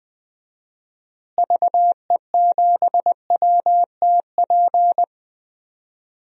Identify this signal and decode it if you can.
Morse code, heard as a keyed tone, about 20 wpm, 700 Hz. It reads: VE7WTP
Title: DXCC-100 Call Signs - 6 Characters - Repeated 3x - 20wpm